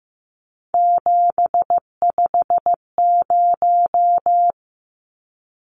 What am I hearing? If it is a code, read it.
Morse code, heard as a keyed tone, about 15 wpm, 700 Hz. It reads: 750
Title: Numbers - 3-Digit 15wpm